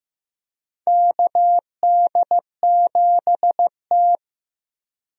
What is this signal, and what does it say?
Morse code, heard as a keyed tone, about 15 wpm, 700 Hz. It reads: KD7T
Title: USA Call Signs - 4 Characters - Repeated 2x - 15wpm